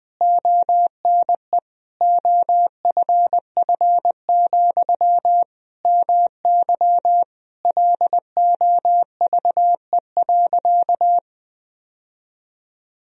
Morse code, heard as a keyed tone, about 21 wpm, 700 Hz. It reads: ONE OFF, MY LOVE.
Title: Sentences from Top 1000 Words - Review Part 2 - 20wpm